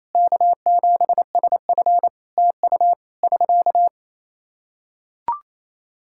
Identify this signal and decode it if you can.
Morse code, heard as a keyed tone, about 28 wpm, 700 Hz. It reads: K7SF TU <SK>
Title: Practice QSO - Exchange 4 - Ending - 28wpm